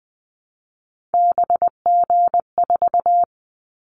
Morse code, heard as a keyed tone, about 20 wpm, 700 Hz. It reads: BG4
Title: Callsign Prefixes for Top 100 DXCC - 20wpm